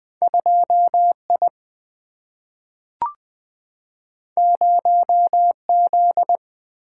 Morse code, heard as a keyed tone, about 20 wpm, 700 Hz. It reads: 2I 0Z
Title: Single-Letter Single-Number 20wpm